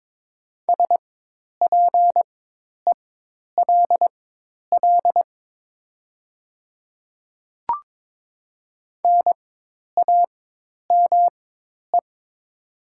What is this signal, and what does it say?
Morse code, heard as a keyed tone, about 22 wpm, spaced as for 10 wpm, 700 Hz. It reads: SPELL NAME
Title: 4x Character Spacing - Top 500 Words - 22wpm